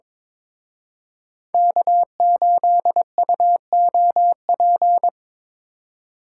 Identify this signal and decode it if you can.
Morse code, heard as a keyed tone, about 26 wpm, 700 Hz. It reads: K8UOP
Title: Speed Racing - USA 1x3 Call Signs 22wpm